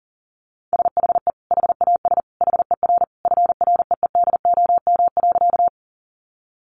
Morse code, heard as a keyed tone, about 40 wpm, 700 Hz. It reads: SHE HAS HER FREEDOM.
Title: Sentences from Top 800 Words 40wpm